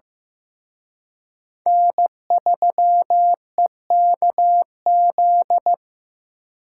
Morse code, heard as a keyed tone, about 18 wpm, 700 Hz. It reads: N3EKZ
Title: Speed Racing - USA 1x3 Call Signs 15wpm